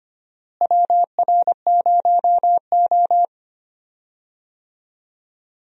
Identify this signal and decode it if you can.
Morse code, heard as a keyed tone, about 25 wpm, 700 Hz. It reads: WR0O
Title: USA Call Signs - 4 Characters - Repeated 3x - 25wpm